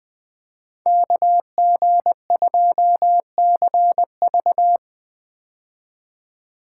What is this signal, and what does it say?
Morse code, heard as a keyed tone, about 20 wpm, 700 Hz. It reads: KG2CV
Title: 2023 POTA Hunter Call Signs - 20wpm